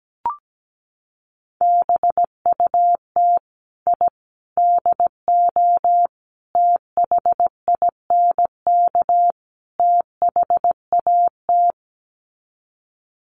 Morse code, heard as a keyed tone, about 17 wpm, 700 Hz. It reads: BUT I DO THINK THAT
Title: Sets of 5 Words Encore2 17wpm